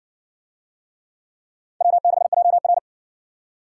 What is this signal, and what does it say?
Morse code, heard as a keyed tone, about 56 wpm, 700 Hz. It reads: W6JR